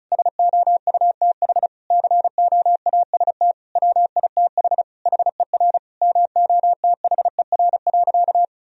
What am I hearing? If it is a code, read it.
Morse code, heard as a keyed tone, about 36 wpm, 700 Hz. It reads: SOUTH COAST WITH HER MOTHER.
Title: Sentences from Top 1000 Words - Review Part 4 - 35wpm